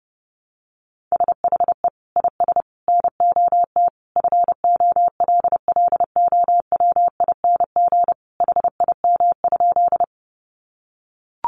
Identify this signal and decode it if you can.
Morse code, heard as a keyed tone, about 30 wpm, 700 Hz. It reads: SHE IS NOT FOLLOWING HIM?
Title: Sentences from Top 700 Words 30wpm